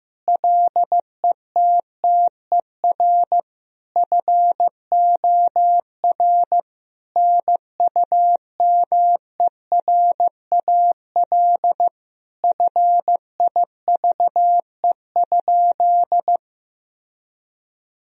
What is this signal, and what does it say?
Morse code, heard as a keyed tone, about 16 wpm, 700 Hz. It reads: LETTER FOR NUMERAL FIVE?
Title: Sentences from Top 1000 Words - Review Part 2 - 15wpm